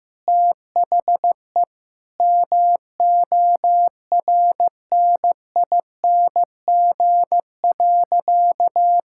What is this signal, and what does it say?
Morse code, heard as a keyed tone, about 15 wpm, 700 Hz. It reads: THE MORNING.
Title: Sentences from Top 1000 Words - Review Part 5 - 15wpm